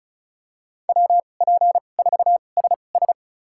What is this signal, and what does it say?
Morse code, heard as a keyed tone, about 36 wpm, spaced as for 25 wpm, 700 Hz. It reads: WP4SS
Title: USA Call Signs - 5 Characters - 2x Character Spacing - 35wpm